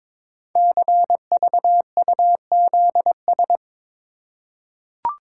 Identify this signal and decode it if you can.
Morse code, heard as a keyed tone, about 22 wpm, 700 Hz. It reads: CVUZS
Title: Code Groups - 5 Letters - 22wpm